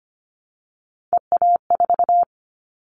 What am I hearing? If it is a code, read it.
Morse code, heard as a keyed tone, about 25 wpm, 700 Hz. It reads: EA4
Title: Callsign Prefixes for Top 25 DXCC - Repeated 3x - 25wpm